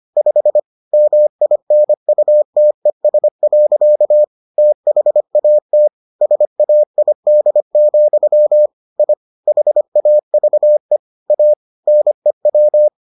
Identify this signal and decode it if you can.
Morse code, heard as a keyed tone, about 25 wpm, 600 Hz. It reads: 5 MINUTES. THAT SAID, I HAVE A NEW